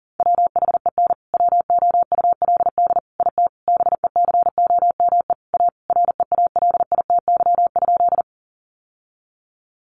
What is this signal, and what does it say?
Morse code, heard as a keyed tone, about 40 wpm, 700 Hz. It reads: WHEN WOULD IT BECOME A REALITY?
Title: Sentences from Top 1200 Words 40wpm